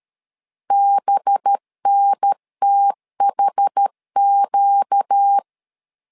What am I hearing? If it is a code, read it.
Morse code, heard as a keyed tone, about 13 wpm, 800 Hz. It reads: BNTHQ
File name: LEZIONI / 5 LBVC / 5 LB 12wpm 1906